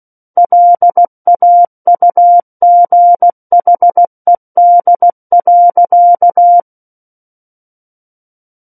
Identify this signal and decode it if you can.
Morse code, heard as a keyed tone, about 16 wpm, 700 Hz. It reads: LAUGHED.
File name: Book One / 07-Chapter 7 - 16 WPM